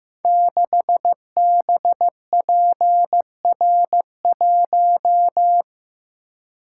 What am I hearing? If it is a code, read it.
Morse code, heard as a keyed tone, about 15 wpm, 700 Hz. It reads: 6BPR1
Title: Code Groups - 5 Letter:Numbers - 15wpm